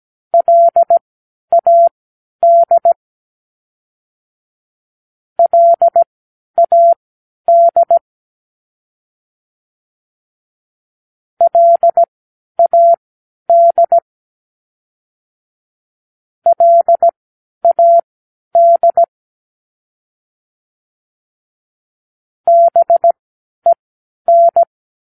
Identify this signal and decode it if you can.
Morse code, heard as a keyed tone, about 17 wpm, spaced as for 10 wpm, 680 Hz. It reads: LAD LAD LAD LAD BEN
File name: k7qo / Practicewords